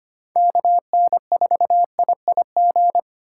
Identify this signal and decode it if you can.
Morse code, heard as a keyed tone, about 31 wpm, 700 Hz. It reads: KN4IIG